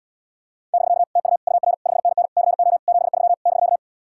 Morse code, heard as a keyed tone, about 77 wpm, 700 Hz. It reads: <BK> TU UR 5NN CO 73 <BK>